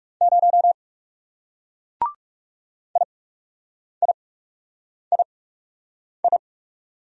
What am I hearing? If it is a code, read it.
Morse code, heard as a keyed tone, about 40 wpm, spaced as for 8 wpm, 700 Hz. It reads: 0 IIII